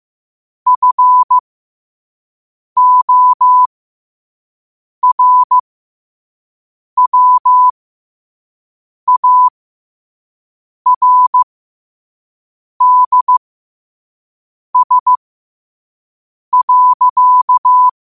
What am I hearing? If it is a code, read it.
Morse code, heard as a keyed tone, about 15 wpm, spaced as for 5 wpm, 1000 Hz. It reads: FORWARDS.
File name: qotdfiles / 5wpm